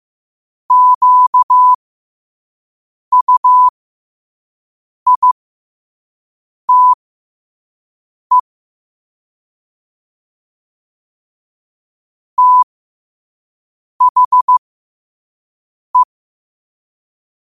Morse code, heard as a keyed tone, about 15 wpm, spaced as for 5 wpm, 1000 Hz. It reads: QUITE THE
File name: qotdfiles / 5wpm